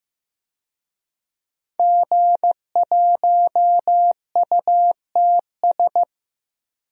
Morse code, heard as a keyed tone, about 15 wpm, 700 Hz. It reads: G1UTS
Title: DXCC-100 Call Signs - 5 Characters - Repeated 2x - 15wpm